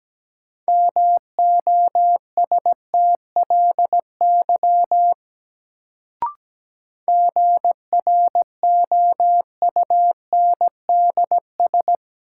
Morse code, heard as a keyed tone, about 17 wpm, 700 Hz. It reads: MOSTLY GROUNDS
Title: Sentences from Top 1300 Words - New Word Drill - 17wpm